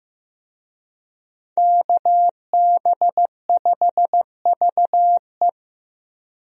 Morse code, heard as a keyed tone, about 15 wpm, 700 Hz. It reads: KB5VE
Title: USA Call Signs - 5 Characters - Repeated 3x - 15wpm